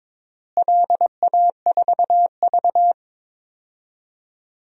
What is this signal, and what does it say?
Morse code, heard as a keyed tone, about 22 wpm, 700 Hz. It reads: LA4V
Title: DXCC-100 Call Signs - 4 Characters - Repeated 3x - 22wpm